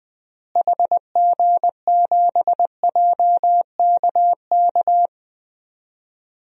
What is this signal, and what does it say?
Morse code, heard as a keyed tone, about 20 wpm, 700 Hz. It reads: HG7JKK
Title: DXCC-100 Call Signs - 6 Characters - Repeated 3x - 20wpm